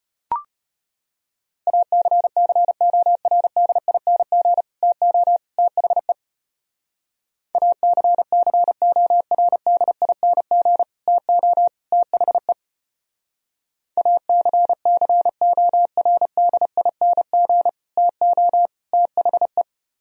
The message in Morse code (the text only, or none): ACCORDING TO THE ACCORDING TO THE ACCORDING TO THE